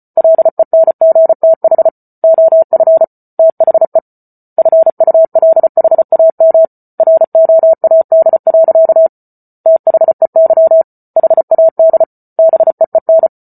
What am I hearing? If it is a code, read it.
Morse code, heard as a keyed tone, about 34 wpm, 650 Hz. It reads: LENGTH OF THE FULHAM ROAD. THEY HAD BEEN